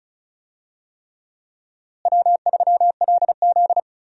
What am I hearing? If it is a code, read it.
Morse code, heard as a keyed tone, about 36 wpm, 700 Hz. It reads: W3LZ
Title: USA Call Signs - 4 Characters - Repeated 3x - 35wpm